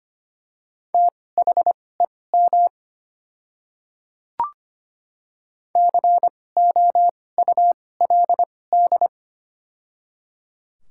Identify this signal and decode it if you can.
Morse code, heard as a keyed tone, about 25 wpm, spaced as for 18 wpm, 700 Hz. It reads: THEM COULD
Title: 2x Character Spacing - Top 300 Words - 25wpm